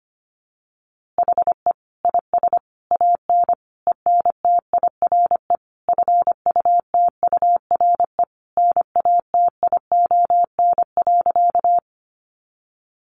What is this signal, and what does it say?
Morse code, heard as a keyed tone, about 25 wpm, 700 Hz. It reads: HE IS AN ENTIRE FUTURE NATION.
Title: A Sound Of Thunder 25wpm